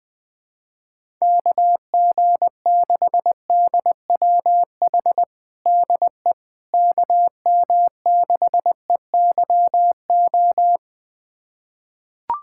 Morse code, heard as a keyed tone, about 21 wpm, 700 Hz. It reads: KG6DWH DE KM6EYO
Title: USA Callsign DE Callsign - 6 Characters - 20wpm